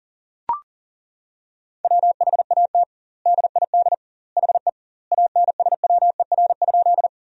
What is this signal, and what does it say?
Morse code, heard as a keyed tone, about 40 wpm, 700 Hz. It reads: WHAT DID HE ANSWER?
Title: 2x Word-Spacing - Sentences from Top 300 Words - 40wpm